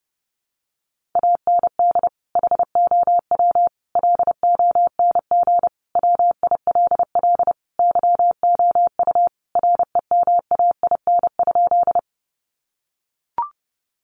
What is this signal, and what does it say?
Morse code, heard as a keyed tone, about 30 wpm, 700 Hz. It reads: AND HOW LONG WILL YOU REMAIN?
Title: Sentences from Top 1200 Words 30wpm